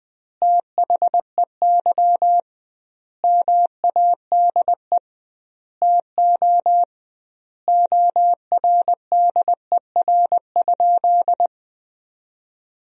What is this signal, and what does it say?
Morse code, heard as a keyed tone, about 20 wpm, 700 Hz. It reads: THEY MADE TO ORDER?
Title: 2x Word-Spacing - Sentences from Top 400 Words - 20wpm